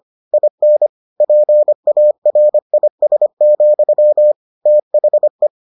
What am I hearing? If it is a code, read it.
Morse code, heard as a keyed tone, about 25 wpm, 600 Hz. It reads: IN PARIS, THE